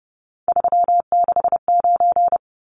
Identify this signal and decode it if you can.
Morse code, heard as a keyed tone, about 30 wpm, 700 Hz. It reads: 369
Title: Numbers - 3-Digit 30wpm